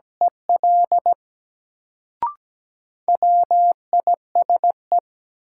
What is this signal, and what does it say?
Morse code, heard as a keyed tone, about 17 wpm, 700 Hz. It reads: EL WISE
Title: Sentences from Top 1600 Words - New Word Drill - 17wpm